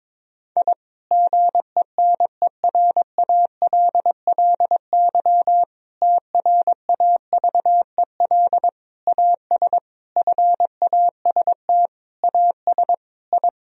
Morse code, heard as a keyed tone, about 22 wpm, 700 Hz. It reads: I GENERALLY TRAVEL AS FAST AS I
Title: Sentences from Top 1000 Words - Review Part 4 - 22wpm